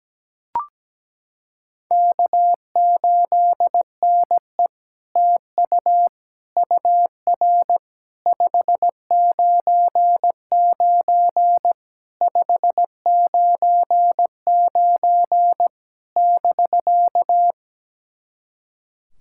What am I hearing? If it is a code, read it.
Morse code, heard as a keyed tone, about 17 wpm, 700 Hz. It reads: K8NE TU UR 599 599 <BK>
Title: SOTA QSO Practice 17wpm